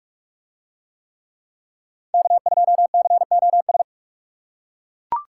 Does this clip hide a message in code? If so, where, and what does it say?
Yes: Morse code from 2.1 to 3.8 s, heard as a keyed tone, about 41 wpm, 700 Hz. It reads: K2COS